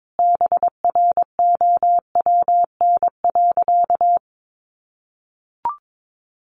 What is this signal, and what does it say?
Morse code, heard as a keyed tone, about 22 wpm, 700 Hz. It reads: BROWN.
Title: Sentences from Top 1200 Words 22wpm